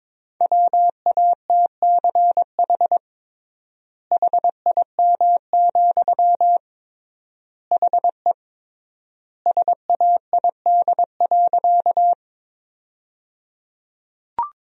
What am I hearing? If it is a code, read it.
Morse code, heard as a keyed tone, about 21 wpm, 700 Hz. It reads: WATCH HIM, HE SAID.
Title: 3x Word-Spacing - Sentences from Top 300 Words - 22wpm